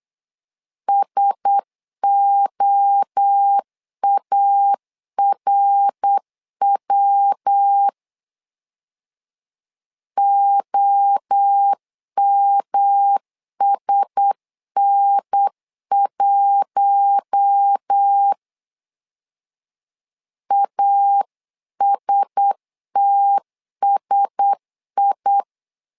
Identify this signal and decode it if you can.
Morse code, heard as a keyed tone, about 8 wpm, 800 Hz. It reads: SOARW OMSN1 ASTSI
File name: LEZIONI / 4 Q1S 8wpm 2710